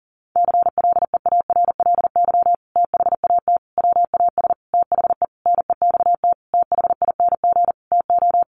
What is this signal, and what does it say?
Morse code, heard as a keyed tone, about 40 wpm, 700 Hz. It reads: CLEARLY THAT WAS THE NEXT THING TO
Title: Sentences from Top 1300 Words 40wpm